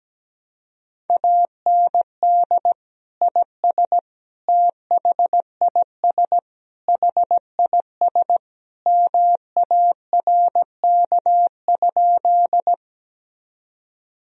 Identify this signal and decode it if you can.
Morse code, heard as a keyed tone, about 17 wpm, 700 Hz. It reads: AND IS THIS HIS MARK?
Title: Sentences from Top 1000 Words - Review Part 1 - 17wpm